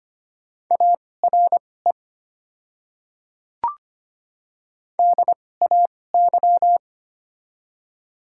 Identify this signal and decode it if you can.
Morse code, heard as a keyed tone, about 25 wpm, spaced as for 18 wpm, 700 Hz. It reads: ARE DAY
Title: 2x Character Spacing - Top 100 Words - 25wpm